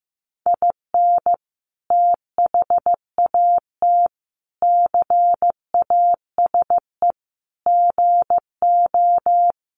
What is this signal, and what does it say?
Morse code, heard as a keyed tone, about 15 wpm, 700 Hz. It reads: IN THAT CASE GO
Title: Sentences from Top 700 Words 15wpm